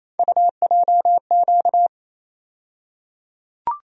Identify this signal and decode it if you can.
Morse code, heard as a keyed tone, about 28 wpm, 700 Hz. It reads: UJQ
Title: USA Call Sign Postfixes - 3 Letters 28wpm